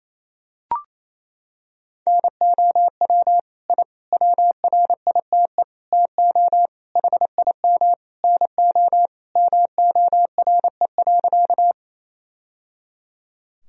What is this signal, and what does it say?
Morse code, heard as a keyed tone, about 28 wpm, 700 Hz. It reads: NOW I WRITE TO HIM NO MORE.
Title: Sentences from Top 1000 Words - Review Part 1 - 28wpm